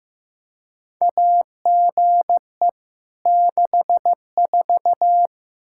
Morse code, heard as a keyed tone, about 15 wpm, 700 Hz. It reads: AGE 64
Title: Practice QSO - Exchange 3 - More Info - 15wpm